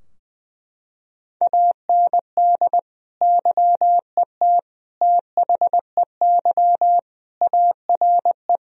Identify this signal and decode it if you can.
Morse code, heard as a keyed tone, about 20 wpm, 700 Hz. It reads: AND YET THEY ARE
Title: Sentences from Top 1000 Words - Review Part 4 - 20wpm